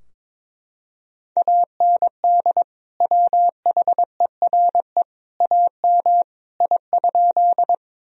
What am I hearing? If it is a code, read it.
Morse code, heard as a keyed tone, about 22 wpm, 700 Hz. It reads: AND WHERE AM I?
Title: Sentences from Top 1000 Words - Review Part 2 - 22wpm